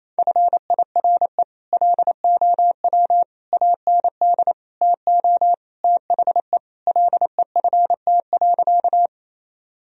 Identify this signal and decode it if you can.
Morse code, heard as a keyed tone, about 28 wpm, 700 Hz. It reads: FIRE LOW AND TO THE LEFT.
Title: Sentences from Top 1000 Words - Review Part 1 - 28wpm